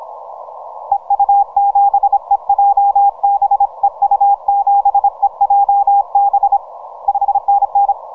Morse code, heard as a keyed tone, about 26 wpm, 800 Hz, with radio noise. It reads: EU7EJB EU7EJB 5NN
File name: VK0EK 12MCW